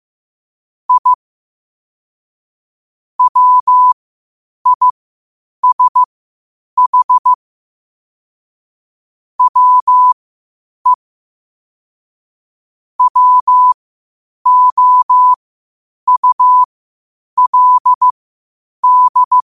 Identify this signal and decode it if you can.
Morse code, heard as a keyed tone, about 15 wpm, spaced as for 8 wpm, 1000 Hz. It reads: I WISH WE WOULD
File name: qotdfiles / 8wpm